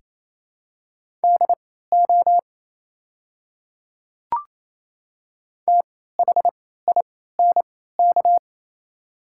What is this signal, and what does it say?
Morse code, heard as a keyed tone, about 28 wpm, spaced as for 16 wpm, 700 Hz. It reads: DO THINK
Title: 3x Character Spacing - Top 200 Words - 28wpm